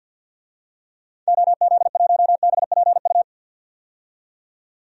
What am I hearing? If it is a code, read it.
Morse code, heard as a keyed tone, about 46 wpm, 700 Hz. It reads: OZ1BPU
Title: DXCC-100 Call Signs - 6 Characters - 50wpm